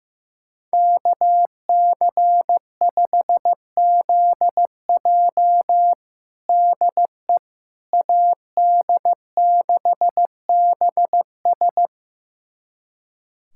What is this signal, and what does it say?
Morse code, heard as a keyed tone, about 15 wpm, 700 Hz. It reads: KC5ZJ DE AD6BS
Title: USA Callsign DE Callsign - 5 Characters - 15wpm